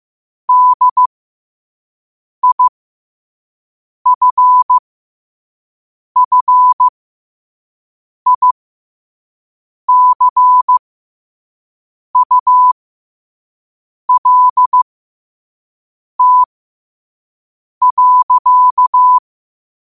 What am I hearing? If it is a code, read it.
Morse code, heard as a keyed tone, about 15 wpm, spaced as for 5 wpm, 1000 Hz. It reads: DIFFICULT.
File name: qotdfiles / 5wpm